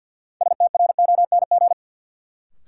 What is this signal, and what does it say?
Morse code, heard as a keyed tone, about 53 wpm, 700 Hz. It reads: STRONG
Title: Sentences from Top 500 Words - New Word Drill - 50wpm